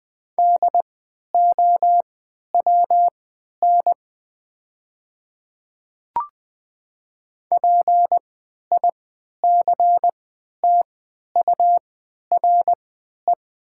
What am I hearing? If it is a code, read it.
Morse code, heard as a keyed tone, about 20 wpm, spaced as for 11 wpm, 700 Hz. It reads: DOWN PICTURE
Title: 3x Character Spacing - Top 200 Words - 20wpm